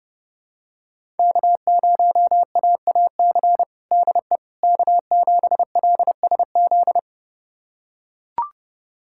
Morse code, heard as a keyed tone, about 30 wpm, 700 Hz. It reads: K0AAC DE K7LSZ
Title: USA Callsign DE Callsign - 5 Characters - 30wpm